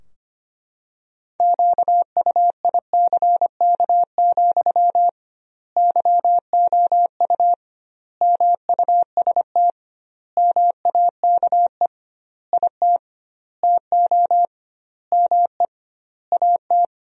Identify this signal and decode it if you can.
Morse code, heard as a keyed tone, about 25 wpm, 700 Hz. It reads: QUICK, YOU MUST MAKE IT TO ME AT
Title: 2x Word-Spacing - Sentences from Top 500 Words - 25wpm